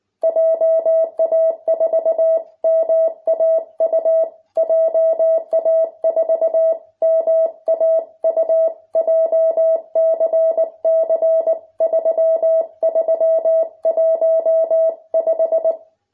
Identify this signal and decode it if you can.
Morse code, heard as a keyed tone, about 19 wpm, 650 Hz. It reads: JA4MAU JA4MAU JCC3315